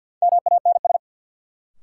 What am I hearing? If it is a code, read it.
Morse code, heard as a keyed tone, about 47 wpm, 700 Hz. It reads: MANS
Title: Sentences from Top 1000 Words - Review Part 6 - New Word Drill - 50wpm